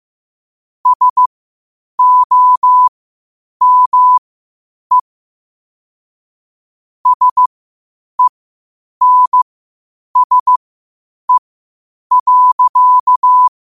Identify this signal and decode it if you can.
Morse code, heard as a keyed tone, about 15 wpm, spaced as for 8 wpm, 1000 Hz. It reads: SOME SENSE.
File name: qotdfiles / 8wpm